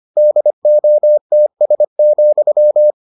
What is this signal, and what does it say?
Morse code, heard as a keyed tone, about 25 wpm, 600 Hz. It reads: DOTS,